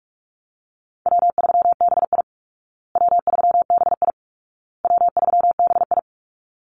Morse code, heard as a keyed tone, about 44 wpm, 700 Hz. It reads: W3BI W3BI W3BI
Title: Easy Call Signs Repeated 3x - 45wpm